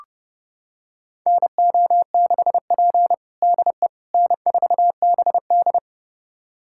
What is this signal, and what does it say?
Morse code, heard as a keyed tone, about 32 wpm, 700 Hz. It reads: NO6P DE N4BD
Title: USA Callsign DE Callsign - 4 Characters - 30wpm